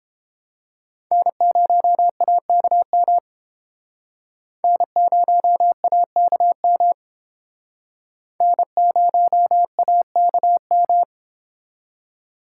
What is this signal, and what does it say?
Morse code, heard as a keyed tone, about 31 wpm, 700 Hz. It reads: N0AKM N0AKM N0AKM